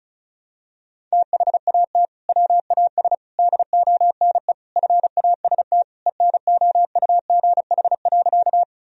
Morse code, heard as a keyed tone, about 36 wpm, 700 Hz. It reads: THAT WAS DONE FAST ENOUGH.
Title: Sentences from Top 1000 Words - Review Part 2 - 35wpm